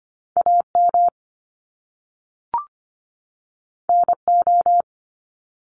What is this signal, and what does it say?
Morse code, heard as a keyed tone, about 25 wpm, 700 Hz. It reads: AM NO
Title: Top Two Letter Combinations in Words 25wpm